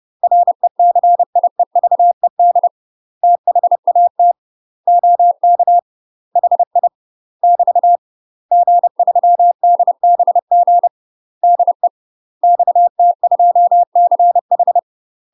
Morse code, heard as a keyed tone, about 30 wpm, 700 Hz. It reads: RECIEVED THAT OK HI <BT> G3DBG DE XT2CH
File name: UK MASTER QSOs TRAINING / 30WPM / UK MASTER QSOs 0023